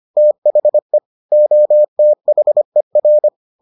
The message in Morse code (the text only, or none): THE OTHER